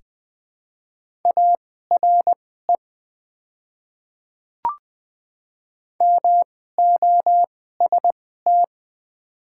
Morse code, heard as a keyed tone, about 20 wpm, spaced as for 14 wpm, 700 Hz. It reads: ARE MOST
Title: 2x Character Spacing - Top 100 Words - 20wpm